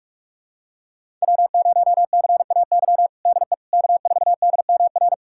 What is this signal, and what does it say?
Morse code, heard as a keyed tone, about 46 wpm, 700 Hz. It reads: W0CAY DE K4DMR
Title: USA Callsign DE Callsign - 5 Characters - 45wpm